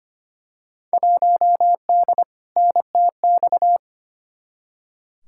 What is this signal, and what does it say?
Morse code, heard as a keyed tone, about 25 wpm, 700 Hz. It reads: 1D NTX
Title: Field Day Exchanges - 25wpm